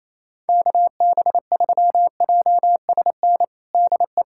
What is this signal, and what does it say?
Morse code, heard as a keyed tone, about 28 wpm, 700 Hz. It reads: KB3JSN DE
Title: USA Callsign DE Callsign - 6 Characters - 28wpm